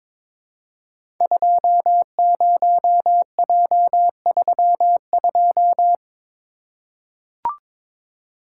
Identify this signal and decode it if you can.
Morse code, heard as a keyed tone, about 23 wpm, 700 Hz. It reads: 20J32
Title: Code Groups - 5 Letter:Numbers - 22wpm